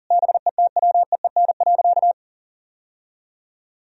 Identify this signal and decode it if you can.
Morse code, heard as a keyed tone, about 40 wpm, 700 Hz. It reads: BETWEEN.
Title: Sentences from Top 1000 Words - Review Part 1 - 40wpm